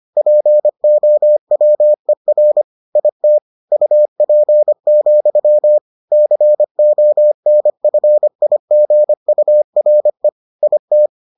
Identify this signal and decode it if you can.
Morse code, heard as a keyed tone, about 25 wpm, 600 Hz. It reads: POWER IT UP, CONFIGURE IT